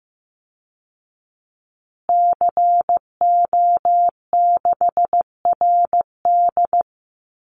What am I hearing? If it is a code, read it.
Morse code, heard as a keyed tone, about 15 wpm, 700 Hz. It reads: CO6RD
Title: Call Signs Repeated 2x - 15wpm